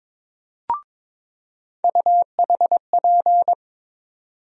Morse code, heard as a keyed tone, about 22 wpm, 700 Hz. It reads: UHP
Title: USA Call Sign Postfixes - 3 Letters 22wpm